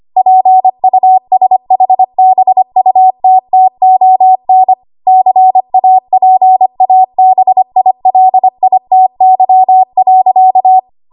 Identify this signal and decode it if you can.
Morse code, heard as a keyed tone, about 25 wpm, 750 Hz. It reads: PUSHBUTTON CAPABILITY.
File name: Archive / 25 WPM / 25WPM